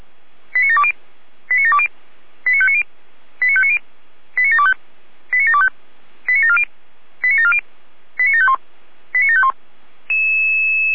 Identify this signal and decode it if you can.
Five-tone selective call (ZVEI), heard as quick runs of five tones, tones of about 70 ms. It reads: 89720 89720 89590 89590 89725 89725 89740 89740 89751 89751
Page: SDR/ZVEI